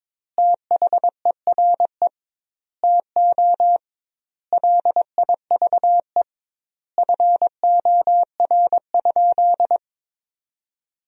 Morse code, heard as a keyed tone, about 22 wpm, 700 Hz. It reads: THERE TO LIVE FOR?
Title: 2x Word-Spacing - Sentences from Top 200 Words - 22wpm